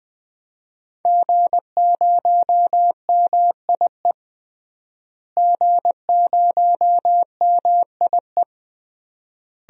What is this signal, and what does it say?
Morse code, heard as a keyed tone, about 20 wpm, 700 Hz. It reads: G0MIE G0MIE
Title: DXCC-100 Call Signs - 5 Characters - Repeated 3x - 20wpm